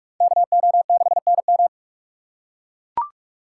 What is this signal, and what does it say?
Morse code, heard as a keyed tone, about 41 wpm, 700 Hz. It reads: KO6NM